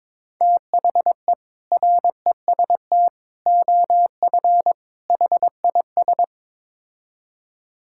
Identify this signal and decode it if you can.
Morse code, heard as a keyed tone, about 22 wpm, 700 Hz. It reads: THE REST OF HIS
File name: practice / 22wpm / Build A Phrase - 5-Word Phrases - 22wpm